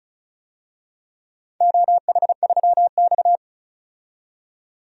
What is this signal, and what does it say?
Morse code, heard as a keyed tone, about 36 wpm, 700 Hz. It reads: OH3X
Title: DXCC-100 Call Signs - 4 Characters - 35wpm